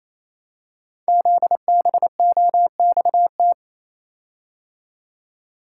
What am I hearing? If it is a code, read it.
Morse code, heard as a keyed tone, about 28 wpm, 700 Hz. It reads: ZBOXT
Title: Code Groups - 5 Letters - 28wpm